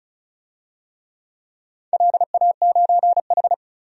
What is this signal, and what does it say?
Morse code, heard as a keyed tone, about 34 wpm, 700 Hz. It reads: LA9H
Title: DXCC-100 Call Signs - 4 Characters - 35wpm